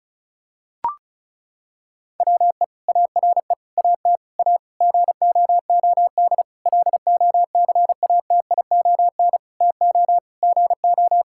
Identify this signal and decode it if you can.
Morse code, heard as a keyed tone, about 36 wpm, 700 Hz. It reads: WE ARE AT A GOOD LOCATION TO GO
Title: Sentences from Top 1000 Words - Review Part 6 - 35wpm